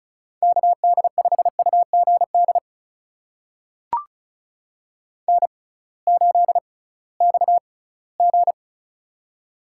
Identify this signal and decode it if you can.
Morse code, heard as a keyed tone, about 36 wpm, spaced as for 20 wpm, 700 Hz. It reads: KD5UGD N8XG
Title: USA Call Signs - 6x Character Spacing - 35wpm